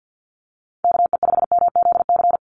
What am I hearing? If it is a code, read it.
Morse code, heard as a keyed tone, about 50 wpm, 700 Hz. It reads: KE5MZC
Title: USA Call Signs 50wpm